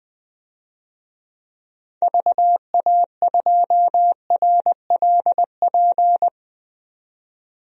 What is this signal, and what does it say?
Morse code, heard as a keyed tone, about 20 wpm, 700 Hz. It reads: VA2RLP
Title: DXCC-100 Call Signs - 6 Characters - 20wpm